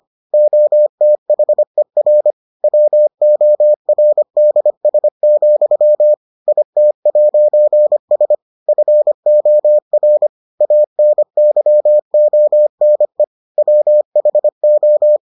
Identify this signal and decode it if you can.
Morse code, heard as a keyed tone, about 25 wpm, 600 Hz. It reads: OTHER WORDS, IT'S FOR ANYONE WHO